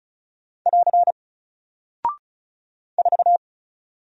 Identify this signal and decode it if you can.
Morse code, heard as a keyed tone, about 36 wpm, 700 Hz. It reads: <AR> 4
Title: Single Character (Letter:Number:Punctuation:Prosign) 35wpm